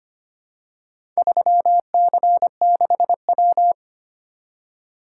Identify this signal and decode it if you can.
Morse code, heard as a keyed tone, about 25 wpm, 700 Hz. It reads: 3C6W
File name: practice / 25wpm / Four Letter-Number Combinations - 25wpm